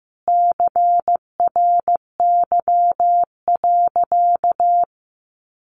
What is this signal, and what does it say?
Morse code, heard as a keyed tone, about 15 wpm, 700 Hz. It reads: CRY.
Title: Sentences from Top 1500 Words 15wpm